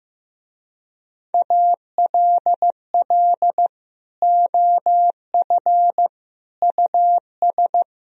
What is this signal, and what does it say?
Morse code, heard as a keyed tone, about 15 wpm, 700 Hz. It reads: ALL OF US
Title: Speed-Racing - Sets of 3 Words 15wpm